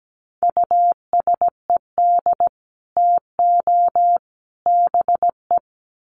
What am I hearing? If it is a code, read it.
Morse code, heard as a keyed tone, about 17 wpm, 700 Hz. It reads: USED TO BE
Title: Sets of 3 Words 17wpm